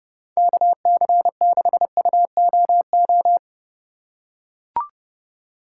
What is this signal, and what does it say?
Morse code, heard as a keyed tone, about 30 wpm, 700 Hz. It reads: KC6UOO